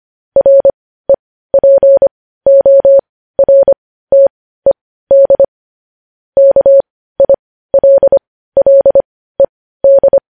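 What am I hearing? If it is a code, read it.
Morse code, heard as a keyed tone, about 25 wpm, spaced as for 15 wpm, 550 Hz. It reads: REPORTED KILLED
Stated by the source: Morse news headlines practice at 15 wpm.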